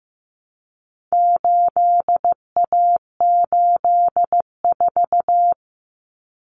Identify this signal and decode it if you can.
Morse code, heard as a keyed tone, about 15 wpm, 700 Hz. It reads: 8A84
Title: Common Cut Numbers - 4-Digit 15wpm